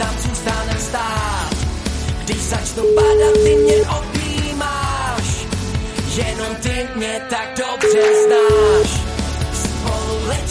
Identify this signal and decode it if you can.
Ringback tone (Europe, 425 Hz), with noise.